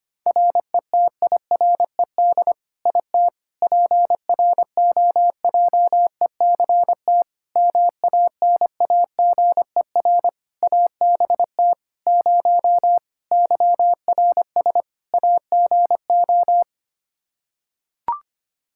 Morse code, heard as a keyed tone, about 25 wpm, 700 Hz. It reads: RETIRED IT PROJECT MANAGER ABT 0 YRS AGO
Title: Practice QSO - Exchange 3 - More Info - 25wpm